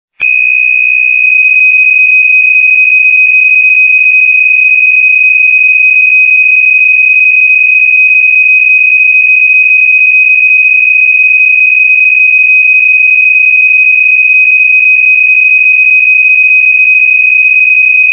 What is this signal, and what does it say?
PSK63, about 62.5 baud.
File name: PSK63-test-message